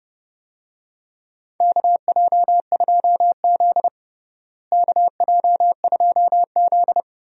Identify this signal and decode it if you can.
Morse code, heard as a keyed tone, about 30 wpm, 700 Hz. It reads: KJ2Z KJ2Z
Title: USA Call Signs - 4 Characters - Repeated 2x - 30wpm